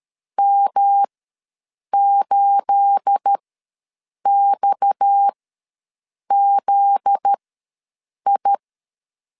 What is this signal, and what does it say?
Morse code, heard as a keyed tone, about 13 wpm, spaced as for 7 wpm, 800 Hz. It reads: M8XZI
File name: LEZIONI / 9 2867 ws 12wpm 3431